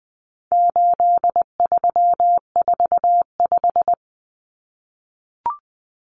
Morse code, heard as a keyed tone, about 20 wpm, 700 Hz. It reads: 8345
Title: Common Cut Numbers - 4-Digit 20wpm